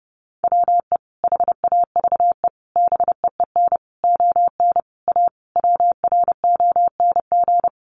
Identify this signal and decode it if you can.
Morse code, heard as a keyed tone, about 30 wpm, 700 Hz. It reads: WE HAVE BEEN ON A WRONG
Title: Sentences from Top 1100 Words 30wpm